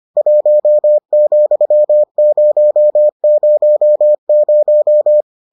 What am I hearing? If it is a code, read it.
Morse code, heard as a keyed tone, about 25 wpm, 600 Hz. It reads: 1,000